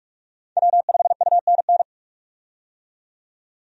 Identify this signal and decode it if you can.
Morse code, heard as a keyed tone, about 42 wpm, 700 Hz. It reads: W5UNN
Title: USA Call Signs - 5 Characters - 45wpm